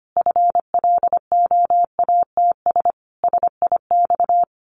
Morse code, heard as a keyed tone, about 25 wpm, 700 Hz. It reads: FLOATS SIX
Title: A Sound Of Thunder 25wpm